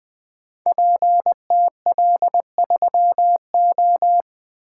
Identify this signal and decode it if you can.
Morse code, heard as a keyed tone, about 20 wpm, 700 Hz. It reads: PTL3O
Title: Code Groups - 5 Letter:Numbers - 20wpm